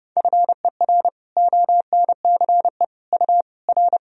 Morse code, heard as a keyed tone, about 32 wpm, 700 Hz. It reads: FER ONCE U R
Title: Sentences from Top 300 Words and Abbreviations 30wpm